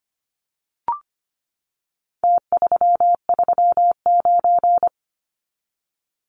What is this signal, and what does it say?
Morse code, heard as a keyed tone, about 25 wpm, 700 Hz. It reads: T339
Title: Numbers - Contest Style - 4-Digit 25wpm